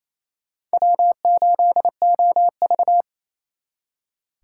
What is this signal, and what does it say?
Morse code, heard as a keyed tone, about 37 wpm, 700 Hz. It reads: W8OV